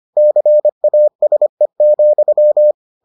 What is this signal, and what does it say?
Morse code, heard as a keyed tone, about 25 wpm, 600 Hz. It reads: CASE,